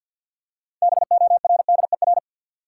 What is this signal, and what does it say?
Morse code, heard as a keyed tone, about 48 wpm, 700 Hz. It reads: BORDER
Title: Six Letter Words - 50wpm